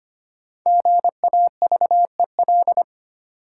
Morse code, heard as a keyed tone, about 25 wpm, 690 Hz. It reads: GAVEL
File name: practice / 25wpm / Easy Words with Hints - 25wpm